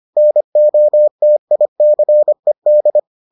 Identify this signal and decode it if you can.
Morse code, heard as a keyed tone, about 25 wpm, 600 Hz. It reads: NOTICED